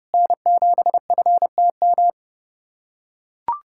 Morse code, heard as a keyed tone, about 30 wpm, 700 Hz. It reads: N7FTM
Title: USA Callsign DE Callsign 30wpm